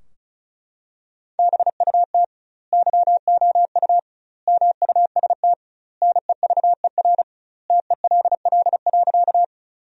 Morse code, heard as a keyed tone, about 34 wpm, 700 Hz. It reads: BUT YOU MUST NEVER TELL.
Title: 2x Word-Spacing - Sentences from Top 300 Words - 35wpm